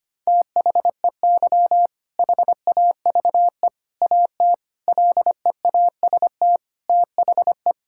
Morse code, heard as a keyed tone, about 25 wpm, 700 Hz. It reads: THEY HAVE AT LEAST THE
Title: Sentences from Top 1000 Words - Review Part 6 - 25wpm